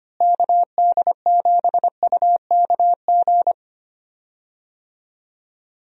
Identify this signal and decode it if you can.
Morse code, heard as a keyed tone, about 25 wpm, 700 Hz. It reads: KD7UKG
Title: USA Call Signs - 6 Characters - Repeated 2x - 25wpm